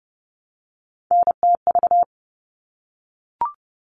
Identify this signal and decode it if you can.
Morse code, heard as a keyed tone, about 30 wpm, 700 Hz. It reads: NTV
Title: All Cut Numbers - 3-Digit 30wpm